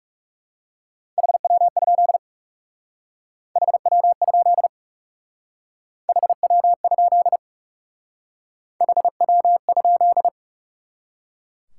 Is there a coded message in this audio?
Yes: Morse code from 1.2 to 10.3 s, heard as a keyed tone, about 38 wpm, 700 Hz. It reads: HW? HW? HW? HW?